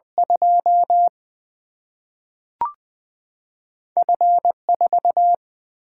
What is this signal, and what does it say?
Morse code, heard as a keyed tone, about 21 wpm, 700 Hz. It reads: 2 F 4
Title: Single-Letter Single-Number 20wpm